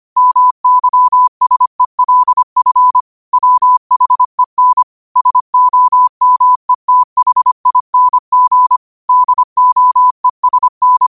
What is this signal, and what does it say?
Morse code, heard as a keyed tone, about 25 wpm, 1000 Hz. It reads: MYSELF WHEN SOMETHING DOESN
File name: qotdfiles / 25wpm